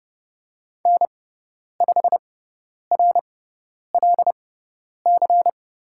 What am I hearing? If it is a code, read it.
Morse code, heard as a keyed tone, about 30 wpm, spaced as for 10 wpm, 700 Hz. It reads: N5RLC
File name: Lesson 17g - Letter u - Call Signs 30wpm